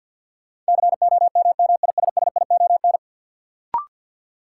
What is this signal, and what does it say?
Morse code, heard as a keyed tone, about 53 wpm, 700 Hz. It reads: COMMISSION
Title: Sentences from Top 1400 Words - New Word Drill - 50wpm